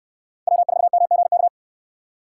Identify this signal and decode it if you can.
Morse code, heard as a keyed tone, about 68 wpm, 700 Hz. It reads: W4MGK